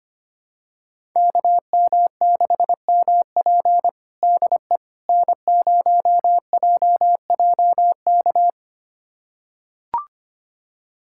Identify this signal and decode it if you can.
Morse code, heard as a keyed tone, about 25 wpm, 700 Hz. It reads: KM6MP DE N0JJK
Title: USA Callsign DE Callsign - 5 Characters - 25wpm